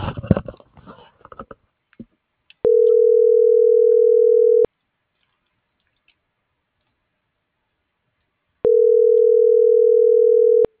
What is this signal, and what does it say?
Ringback tone (North America), 440 + 480 Hz.